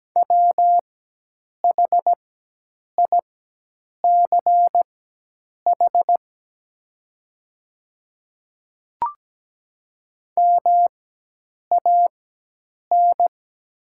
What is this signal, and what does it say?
Morse code, heard as a keyed tone, about 17 wpm, spaced as for 8 wpm, 700 Hz. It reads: WHICH MAN
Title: 4x Character Spacing - Top 500 Words - 17wpm